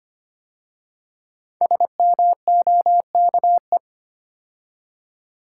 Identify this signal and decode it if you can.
Morse code, heard as a keyed tone, about 25 wpm, 700 Hz. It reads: SMOKE
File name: 5 Letter Words - First 3 Letters Given - 25wpm